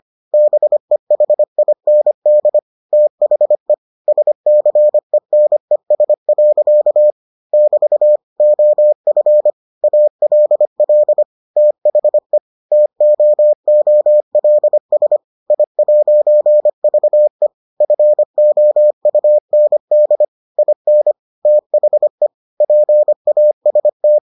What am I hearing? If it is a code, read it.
Morse code, heard as a keyed tone, about 25 wpm, 600 Hz. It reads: BEHIND THE SCENES. <BT> OF ALL THE TOOLS I'VE FOUND IN THE PAST